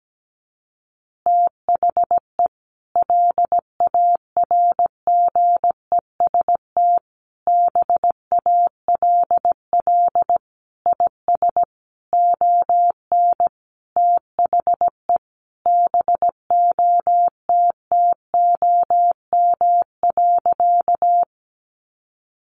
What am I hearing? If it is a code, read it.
Morse code, heard as a keyed tone, about 17 wpm, 700 Hz. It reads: THE LARGEST BALL IS ON THE BOTTOM.
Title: Easy Phrases 17wpm